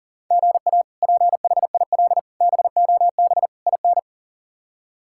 Morse code, heard as a keyed tone, about 38 wpm, 700 Hz. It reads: GA PHIL BOB IN
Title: K1USN SST QSO Practice - 40wpm